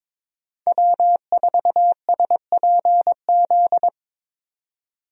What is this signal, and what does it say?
Morse code, heard as a keyed tone, about 21 wpm, 700 Hz. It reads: W4SPZ
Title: USA Call Signs - 5 Characters - 22wpm